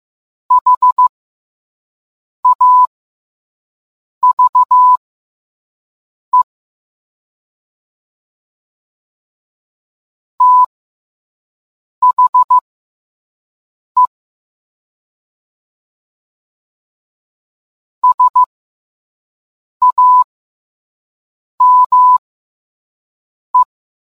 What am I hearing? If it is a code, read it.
Morse code, heard as a keyed tone, about 15 wpm, spaced as for 5 wpm, 1000 Hz. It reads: HAVE THE SAME